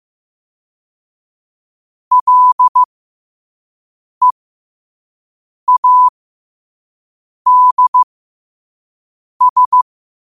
Morse code, heard as a keyed tone, about 15 wpm, spaced as for 5 wpm, 1000 Hz. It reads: LEADS
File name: qotdfiles / 5wpm